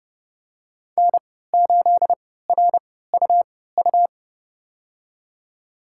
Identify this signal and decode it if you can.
Morse code, heard as a keyed tone, about 29 wpm, spaced as for 17 wpm, 700 Hz. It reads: N8RUU
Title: USA Call Signs - 5 Characters - 3x Character Spacing - 30wpm